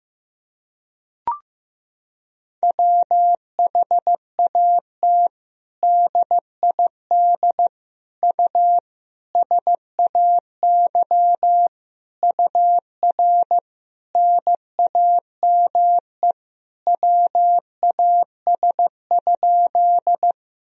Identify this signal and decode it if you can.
Morse code, heard as a keyed tone, about 15 wpm, 700 Hz. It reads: WHAT DID U SAY UR NAME WAS?
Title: Sentences from Top 200 Words and Abbreviations 15wpm